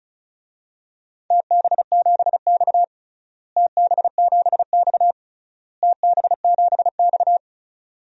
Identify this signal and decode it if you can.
Morse code, heard as a keyed tone, about 36 wpm, 700 Hz. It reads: TB7X TB7X TB7X
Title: DXCC-100 Call Signs - 4 Characters - Repeated 3x - 35wpm